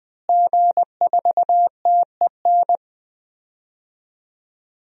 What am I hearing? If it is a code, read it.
Morse code, heard as a keyed tone, about 20 wpm, 700 Hz. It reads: G4TEN
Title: DXCC-100 Call Signs - 5 Characters - Repeated 2x - 20wpm